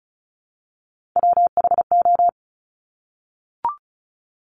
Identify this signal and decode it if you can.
Morse code, heard as a keyed tone, about 34 wpm, 700 Hz. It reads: WHO
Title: Top 200 Words Repeated 35wpm